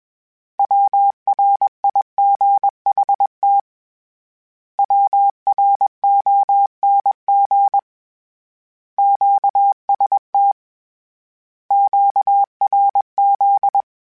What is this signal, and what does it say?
Morse code, heard as a keyed tone, about 21 wpm, 800 Hz. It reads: WRIGHT WRONG QST QRZ
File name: Short Runs / Short Runs-mp3s / Words @20